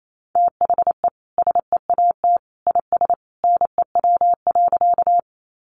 Morse code, heard as a keyed tone, about 28 wpm, 700 Hz. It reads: THE SEAT IS NEW.
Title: Sentences from Top 1500 Words 28wpm